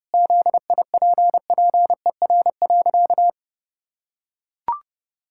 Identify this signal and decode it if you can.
Morse code, heard as a keyed tone, about 30 wpm, 700 Hz. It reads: ZIPPER.
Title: Pangram Challenge v2 - 30wpm